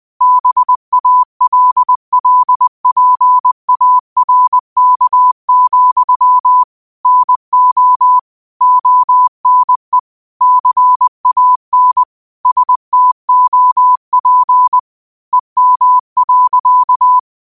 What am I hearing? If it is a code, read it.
Morse code, heard as a keyed tone, about 20 wpm, 1000 Hz. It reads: BALLPARK, NO ONE CAN STOP EM.
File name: qotdfiles / 20wpm